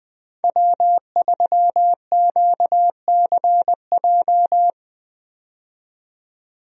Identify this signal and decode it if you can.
Morse code, heard as a keyed tone, about 20 wpm, 700 Hz. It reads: W3QCJ
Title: USA Call Signs - 5 Characters - 20wpm